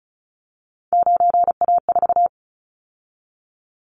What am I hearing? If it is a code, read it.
Morse code, heard as a keyed tone, about 36 wpm, 700 Hz. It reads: 9A4
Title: Callsign Prefixes for Top 100 DXCC - 35wpm